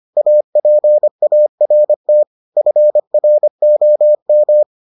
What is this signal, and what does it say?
Morse code, heard as a keyed tone, about 25 wpm, 600 Hz. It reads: APART FROM